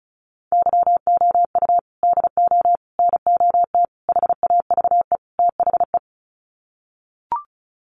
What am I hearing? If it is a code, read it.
Morse code, heard as a keyed tone, about 34 wpm, 700 Hz. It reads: YOU DO NOT HAVE THE
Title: Sets of 5 Words 35wpm